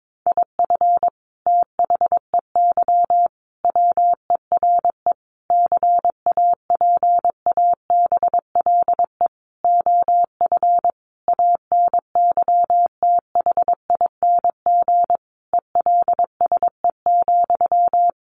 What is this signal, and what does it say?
Morse code, heard as a keyed tone, about 22 wpm, 700 Hz. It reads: IF THEY WERE CAPABLE OF ANYTHING ELSE,
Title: Sentences from Top 1200 Words 22wpm